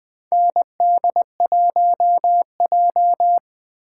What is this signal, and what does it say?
Morse code, heard as a keyed tone, about 20 wpm, 700 Hz. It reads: ND1J
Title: 2023 SOTA Chaser Call Signs - 20wpm